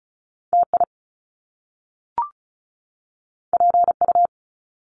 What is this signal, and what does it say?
Morse code, heard as a keyed tone, about 36 wpm, 700 Hz. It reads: TI PU
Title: Top Two Letter Combinations in Words 35wpm